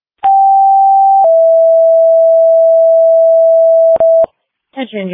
Two-tone page, heard as a one-second tone, then a three-second tone, about 780 then 660 Hz. It is heard from 0.2 to 4.2 s.